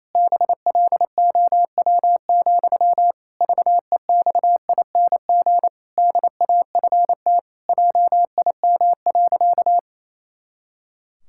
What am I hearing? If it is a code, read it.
Morse code, heard as a keyed tone, about 28 wpm, 700 Hz. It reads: BLOW, VEXING DAFT JIM.
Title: Pangram Challenge v2 - 28wpm